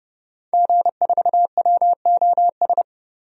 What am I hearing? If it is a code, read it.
Morse code, heard as a keyed tone, about 30 wpm, 700 Hz. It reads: G4WOS
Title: DXCC-100 Call Signs - 5 Characters - 30wpm